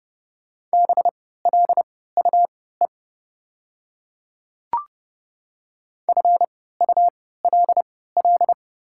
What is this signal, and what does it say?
Morse code, heard as a keyed tone, about 30 wpm, spaced as for 17 wpm, 700 Hz. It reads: BLUE FULL
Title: 3x Character Spacing - Top 500 Words - 30wpm